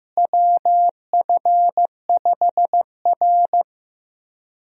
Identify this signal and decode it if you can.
Morse code, heard as a keyed tone, about 15 wpm, 700 Hz. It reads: WF5R
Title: USA Call Signs - 4 Characters - Repeated 3x - 15wpm